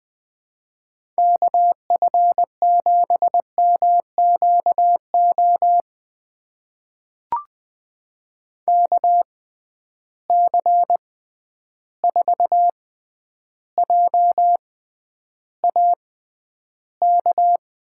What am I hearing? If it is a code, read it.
Morse code, heard as a keyed tone, about 21 wpm, spaced as for 10 wpm, 700 Hz. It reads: KF7MQO KC4JAK
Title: USA Call Signs - 6 Characters - 6x Character Spacing - 20wpm